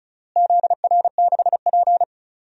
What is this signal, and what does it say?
Morse code, heard as a keyed tone, about 36 wpm, 700 Hz. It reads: ZR6P
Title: DXCC-100 Call Signs - 4 Characters - Repeated 3x - 35wpm